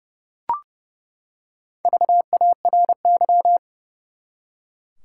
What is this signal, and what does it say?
Morse code, heard as a keyed tone, about 30 wpm, 700 Hz. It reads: VARY
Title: Four Letter Words - 30wpm